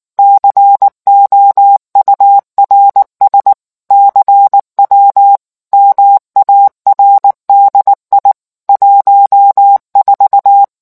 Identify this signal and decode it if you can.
Morse code, heard as a keyed tone, about 19 wpm, 800 Hz. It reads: COURS CW MARDI 14